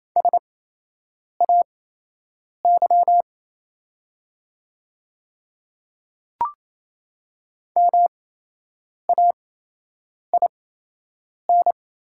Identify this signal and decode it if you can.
Morse code, heard as a keyed tone, about 28 wpm, spaced as for 8 wpm, 700 Hz. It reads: SAY MAIN
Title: 8x Character Spacing - Top 500 Words - 28wpm